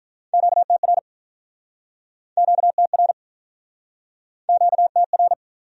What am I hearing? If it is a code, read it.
Morse code, heard as a keyed tone, about 48 wpm, 700 Hz. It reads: QTR QTR QTR